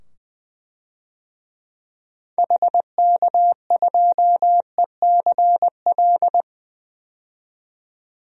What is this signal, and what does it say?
Morse code, heard as a keyed tone, about 21 wpm, 700 Hz. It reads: HK2ECL
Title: DXCC-100 Call Signs - 6 Characters - 20wpm